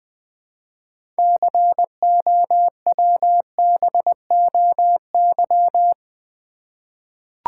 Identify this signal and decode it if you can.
Morse code, heard as a keyed tone, about 20 wpm, 700 Hz. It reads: COWBOY
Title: 6 Letter Words - First 3 Letters Given - 20wpm